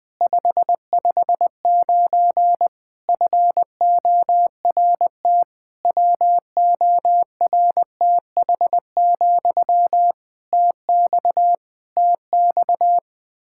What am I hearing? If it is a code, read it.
Morse code, heard as a keyed tone, about 20 wpm, 700 Hz. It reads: 559 FORT WORTH, TX TX
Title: Practice QSO - Exchange 1 - Beginning - 20wpm